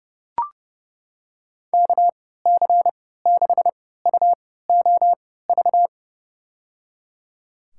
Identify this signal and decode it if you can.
Morse code, heard as a keyed tone, about 30 wpm, spaced as for 17 wpm, 700 Hz. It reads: KC6UOV
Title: USA Call Signs - 6 Characters - 3x Character Spacing - 30wpm